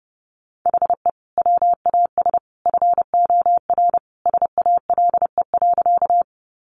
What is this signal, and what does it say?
Morse code, heard as a keyed tone, about 30 wpm, 700 Hz. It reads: HE WAS FOR SALE.
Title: Sentences from Top 1400 Words 30wpm